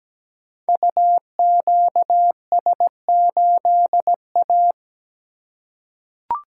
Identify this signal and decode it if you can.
Morse code, heard as a keyed tone, about 17 wpm, 700 Hz. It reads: UQS8A
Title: Code Groups - 5 Letter:Numbers - 17wpm